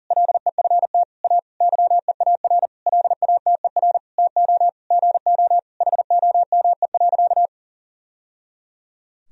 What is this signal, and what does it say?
Morse code, heard as a keyed tone, about 40 wpm, 700 Hz. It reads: LEFT A YEAR LATER TO GO HOME.
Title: Sentences from Top 1000 Words - Review Part 4 - 40wpm